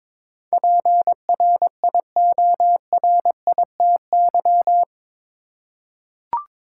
Morse code, heard as a keyed tone, about 22 wpm, 700 Hz. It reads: PRIORITY
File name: practice / 22wpm / Eight Letter Words - 22wpm